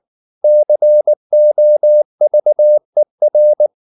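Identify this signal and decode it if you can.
Morse code, heard as a keyed tone, about 19 wpm, 600 Hz. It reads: COVER
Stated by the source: Morse code at 19 wpm from CW med Gnister, reading Ken Follett's novel Never in English.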